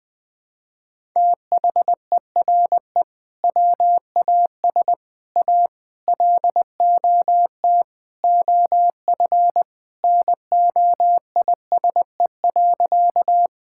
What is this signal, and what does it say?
Morse code, heard as a keyed tone, about 21 wpm, 700 Hz. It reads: THERE WAS A LOT OF NOISE.
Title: Sentences with 4-Word Phrases - 20wpm